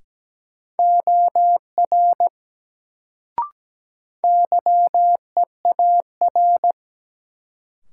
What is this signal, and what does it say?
Morse code, heard as a keyed tone, about 17 wpm, 700 Hz. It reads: OR YEAR
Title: Sets of 2 Words - New Word Drill - 17wpm